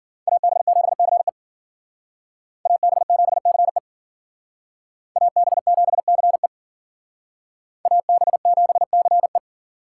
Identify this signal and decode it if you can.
Morse code, heard as a keyed tone, about 50 wpm, 700 Hz. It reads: AB7CE AB7CE AB7CE AB7CE
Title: Speed Racing - USA 2x2 Call Signs 40wpm